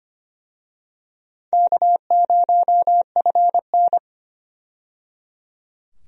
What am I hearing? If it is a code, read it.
Morse code, heard as a keyed tone, about 25 wpm, 700 Hz. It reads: K0FN
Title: USA Call Signs - 4 Characters - 25wpm